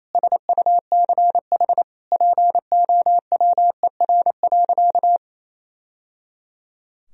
Morse code, heard as a keyed tone, about 28 wpm, 700 Hz. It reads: SUCH POWER.
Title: Sentences from Top 1000 Words - Review Part 2 - 28wpm